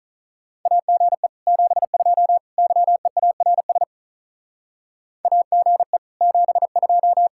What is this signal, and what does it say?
Morse code, heard as a keyed tone, about 44 wpm, 700 Hz. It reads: AGE 72 YEARS AGE 72